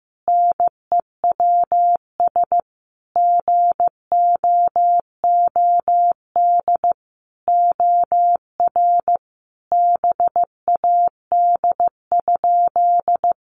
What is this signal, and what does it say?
Morse code, heard as a keyed tone, about 15 wpm, 700 Hz. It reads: NEWS GOOD OR BAD?
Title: Sentences from Top 800 Words 15wpm